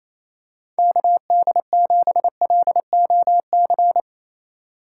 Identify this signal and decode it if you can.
Morse code, heard as a keyed tone, about 28 wpm, 700 Hz. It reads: KD7LOC
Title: USA Call Signs - 6 Characters - 6x Character Spacing - 28wpm